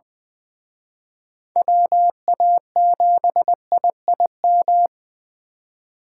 Morse code, heard as a keyed tone, about 24 wpm, 700 Hz. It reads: WA7IIM
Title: Speed Racing - USA 2x3 Call Signs 20wpm